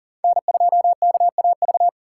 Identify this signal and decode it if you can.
Morse code, heard as a keyed tone, about 40 wpm, 700 Hz. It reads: N2KAV
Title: USA Call Signs - 5 Characters - 6x Character Spacing - 40wpm